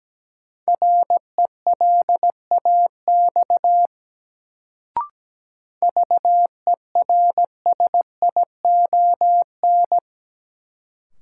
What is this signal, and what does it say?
Morse code, heard as a keyed tone, about 17 wpm, 700 Hz. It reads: RELAX VERSION
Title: Sentences from Top 1400 Words - New Word Drill Alt - 17wpm